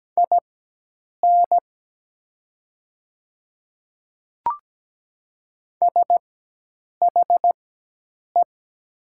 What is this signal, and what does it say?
Morse code, heard as a keyed tone, about 17 wpm, spaced as for 8 wpm, 700 Hz. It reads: IN SHE
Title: 4x Character Spacing - Top 100 Words - 17wpm